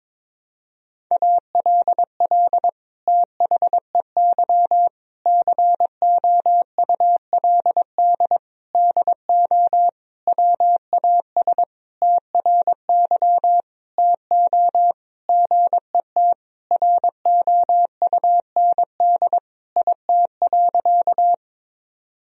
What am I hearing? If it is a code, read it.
Morse code, heard as a keyed tone, about 22 wpm, 700 Hz. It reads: ALL THEY COULD DO WAS TRY TO GET ROUND IT.
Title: Sentences from Top 1000 Words - Review Part 1 - 22wpm